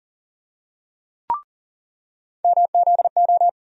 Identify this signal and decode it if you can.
Morse code, heard as a keyed tone, about 38 wpm, 700 Hz. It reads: MZO
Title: USA Call Sign Postfixes - 3 Letters 40wpm